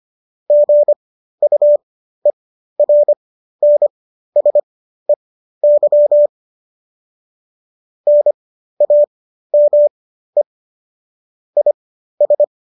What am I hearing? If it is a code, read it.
Morse code, heard as a keyed tone, about 25 wpm, spaced as for 13 wpm, 590 Hz. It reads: GUERNSEY NAME IS